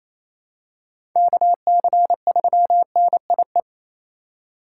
Morse code, heard as a keyed tone, about 28 wpm, 700 Hz. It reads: KC3NIE
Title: K1USN SST QSO Practice - 28wpm